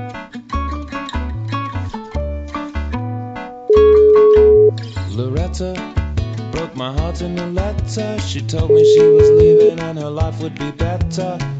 Ringback tone (Europe, 425 Hz).